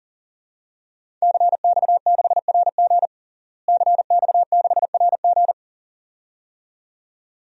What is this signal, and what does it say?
Morse code, heard as a keyed tone, about 40 wpm, 700 Hz. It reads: CX6RG CX6RG
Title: DXCC-100 Call Signs - 5 Characters - Repeated 2x - 40wpm